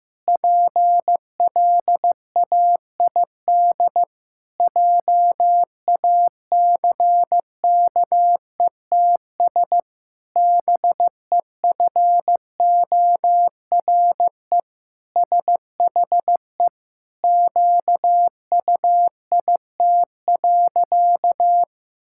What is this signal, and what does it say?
Morse code, heard as a keyed tone, about 15 wpm, 700 Hz. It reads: PLAID JACKETS BEFORE SHE QUIT.
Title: Pangram Challenge v2 - 15wpm